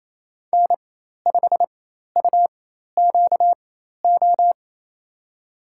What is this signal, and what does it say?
Morse code, heard as a keyed tone, about 28 wpm, spaced as for 13 wpm, 700 Hz. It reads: N5UQO
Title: USA Call Signs - 4x Character Spacing - 28wpm